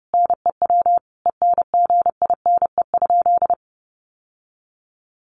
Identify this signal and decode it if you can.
Morse code, heard as a keyed tone, about 30 wpm, 700 Hz. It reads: NEW ENGINE?
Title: Sentences from Top 1500 Words 30wpm